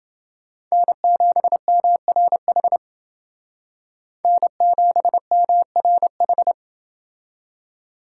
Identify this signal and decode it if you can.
Morse code, heard as a keyed tone, about 28 wpm, 700 Hz. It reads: N7MRH N7MRH